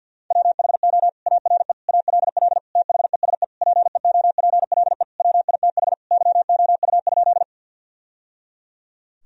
Morse code, heard as a keyed tone, about 49 wpm, 700 Hz. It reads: WHO ARE ALL THESE PEOPLE WITH YOU?
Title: Sentences from Top 1000 Words - Review Part 1 - 50wpm